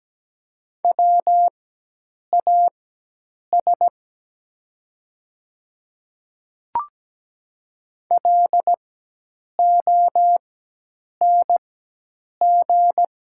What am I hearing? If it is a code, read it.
Morse code, heard as a keyed tone, about 17 wpm, spaced as for 8 wpm, 700 Hz. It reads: WAS LONG
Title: 4x Character Spacing - Top 100 Words - 17wpm